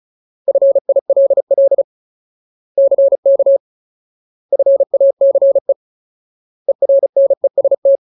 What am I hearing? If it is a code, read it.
Morse code, heard as a keyed tone, about 34 wpm, 550 Hz. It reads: FILL CK FACE ERNEST
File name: Ultimate Word List / 35WPM Ultimate